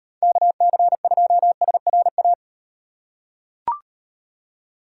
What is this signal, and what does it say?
Morse code, heard as a keyed tone, about 35 wpm, 700 Hz. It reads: KC2SRA